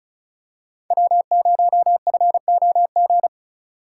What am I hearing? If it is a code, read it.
Morse code, heard as a keyed tone, about 36 wpm, 700 Hz. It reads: W0FOG
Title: USA Call Signs - 5 Characters - 4x Character Spacing - 35wpm